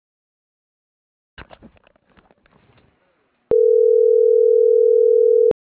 Ringback tone (North America), 440 + 480 Hz.